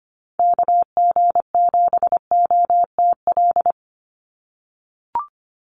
Morse code, heard as a keyed tone, about 25 wpm, 700 Hz. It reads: KG7OTL
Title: USA Call Signs 25wpm